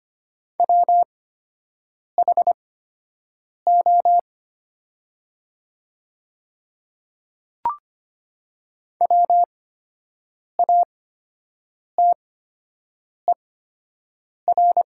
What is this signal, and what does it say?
Morse code, heard as a keyed tone, about 25 wpm, spaced as for 7 wpm, 700 Hz. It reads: WHO WATER
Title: 8x Character Spacing - Top 100 Words - 25wpm